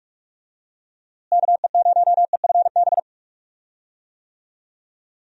Morse code, heard as a keyed tone, about 46 wpm, 700 Hz. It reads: KE0EFB
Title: USA Call Signs - 6 Characters - Repeated 3x - 45wpm